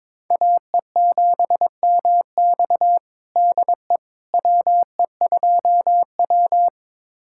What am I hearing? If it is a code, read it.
Morse code, heard as a keyed tone, about 22 wpm, 700 Hz. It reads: AE7MX DE WE2W
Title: USA Callsign DE Callsign 22wpm